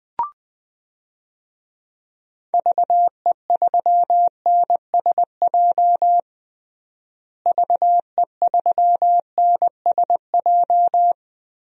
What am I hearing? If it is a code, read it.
Morse code, heard as a keyed tone, about 19 wpm, 700 Hz. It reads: VE3NSJ VE3NSJ
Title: DXCC-100 Call Signs - 6 Characters - Repeated 3x - 20wpm